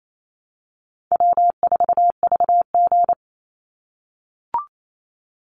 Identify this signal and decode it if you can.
Morse code, heard as a keyed tone, about 28 wpm, 700 Hz. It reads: W4VG
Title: Easy Call Signs - 28wpm